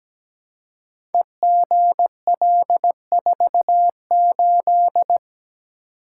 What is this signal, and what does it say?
Morse code, heard as a keyed tone, about 17 wpm, 700 Hz. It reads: EGL48
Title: Code Groups - 5 Letter:Numbers - 17wpm